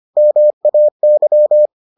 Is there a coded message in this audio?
Yes: Morse code from 0.2 to 1.7 s, heard as a keyed tone, about 25 wpm, 600 Hz. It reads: MAY